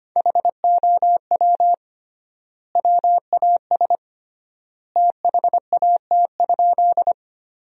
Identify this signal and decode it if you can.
Morse code, heard as a keyed tone, about 25 wpm, 700 Hz. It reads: HOW WAS THAT?
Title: 3x Word-Spacing - Sentences from Top 100 Words - 25wpm